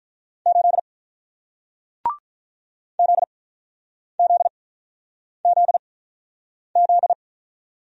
Morse code, heard as a keyed tone, about 44 wpm, spaced as for 9 wpm, 700 Hz. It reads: 8 ZZZZ